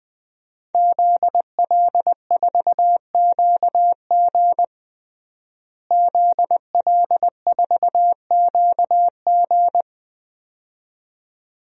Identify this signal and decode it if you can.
Morse code, heard as a keyed tone, about 20 wpm, 700 Hz. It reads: ZL4QG ZL4QG
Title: DXCC-100 Call Signs - 5 Characters - Repeated 2x - 20wpm